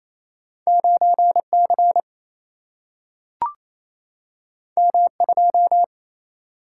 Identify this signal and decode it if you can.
Morse code, heard as a keyed tone, about 28 wpm, 700 Hz. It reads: 9 C M 2
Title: Single-Letter Single-Number 28wpm